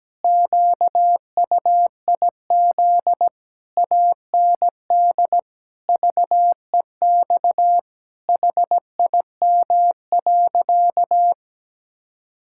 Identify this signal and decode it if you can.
Morse code, heard as a keyed tone, about 17 wpm, 700 Hz. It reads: QUIZ AND VEX HIM.
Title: Pangram Challenge v2 - 17wpm